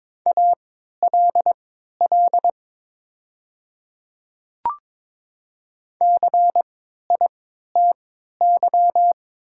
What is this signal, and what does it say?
Morse code, heard as a keyed tone, about 22 wpm, spaced as for 12 wpm, 690 Hz. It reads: ALL CITY
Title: 3x Character Spacing - Top 300 Words - 22wpm